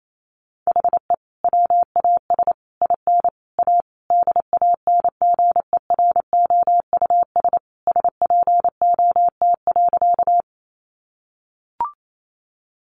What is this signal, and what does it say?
Morse code, heard as a keyed tone, about 28 wpm, 700 Hz. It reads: HE WAS IN A DANGEROUS SPOT.
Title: Sentences from Top 1200 Words 28wpm